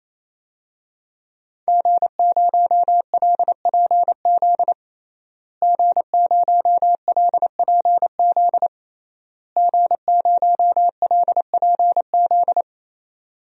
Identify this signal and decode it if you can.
Morse code, heard as a keyed tone, about 28 wpm, 700 Hz. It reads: G0LPZ G0LPZ G0LPZ
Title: DXCC-100 Call Signs - 5 Characters - Repeated 3x - 28wpm